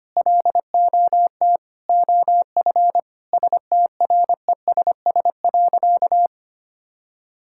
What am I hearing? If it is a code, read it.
Morse code, heard as a keyed tone, about 25 wpm, 700 Hz. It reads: LOT OF STRESS.
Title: Sentences from Top 1000 Words - Review Part 6 - 25wpm